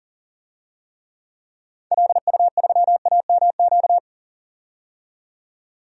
Morse code, heard as a keyed tone, about 40 wpm, 700 Hz. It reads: LU3AMQ
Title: DXCC-100 Call Signs - 6 Characters - 40wpm